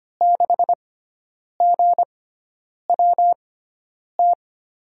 Morse code, heard as a keyed tone, about 25 wpm, spaced as for 9 wpm, 700 Hz. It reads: 6GWT
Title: USA Call Signs - 5 Characters - 6x Character Spacing - 25wpm